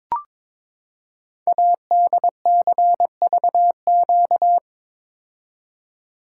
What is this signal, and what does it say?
Morse code, heard as a keyed tone, about 22 wpm, 700 Hz. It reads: ADCVQ
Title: Code Groups - 5 Letters - 22wpm